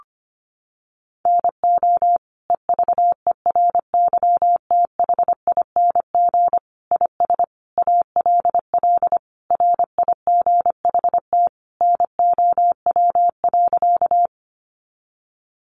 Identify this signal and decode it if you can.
Morse code, heard as a keyed tone, about 25 wpm, 700 Hz. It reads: NO EVERYTHING IS ALL RIGHT NOW.
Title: Sentences from Top 700 Words 25wpm